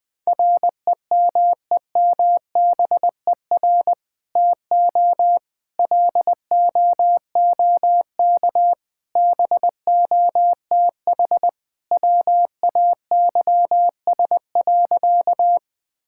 Morse code, heard as a keyed tone, about 21 wpm, 700 Hz. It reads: REMEMBER TO LOOK BOTH WAYS.
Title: Sentences with 2-Word Phrases - 20wpm